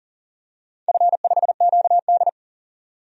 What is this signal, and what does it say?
Morse code, heard as a keyed tone, about 41 wpm, 700 Hz. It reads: F5QD
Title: DXCC-100 Call Signs - 4 Characters - 40wpm